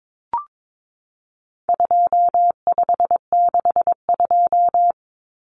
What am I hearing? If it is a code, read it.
Morse code, heard as a keyed tone, about 22 wpm, 700 Hz. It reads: 2562
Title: Common Cut Numbers - 4-Digit 22wpm